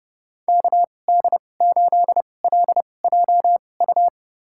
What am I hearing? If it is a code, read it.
Morse code, heard as a keyed tone, about 30 wpm, spaced as for 22 wpm, 700 Hz. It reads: KD8LJU
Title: USA Call Signs - 6 Characters - 2x Character Spacing - 30wpm